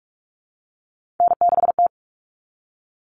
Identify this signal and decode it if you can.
Morse code, heard as a keyed tone, about 44 wpm, 700 Hz. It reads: N6T
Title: All Cut Numbers - 3-Digit 45wpm